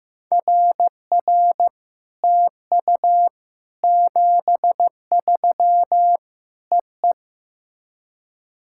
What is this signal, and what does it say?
Morse code, heard as a keyed tone, about 16 wpm, 700 Hz. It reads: RR TU 73 EE
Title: SOTA QSO Practice 15wpm